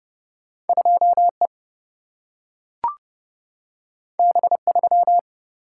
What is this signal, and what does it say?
Morse code, heard as a keyed tone, about 30 wpm, 700 Hz. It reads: 2 E B 3
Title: Single-Letter Single-Number 30wpm